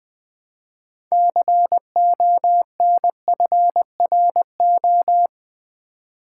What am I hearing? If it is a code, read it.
Morse code, heard as a keyed tone, about 20 wpm, 700 Hz. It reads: CONFRO
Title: Build A Word - 2 Letters At A Time - 8-Letter Words - 20wpm